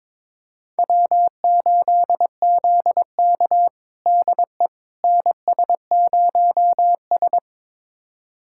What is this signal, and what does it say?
Morse code, heard as a keyed tone, about 22 wpm, 700 Hz. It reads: W8ZK DE NS0S
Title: USA Callsign DE Callsign - 4 Characters - 22wpm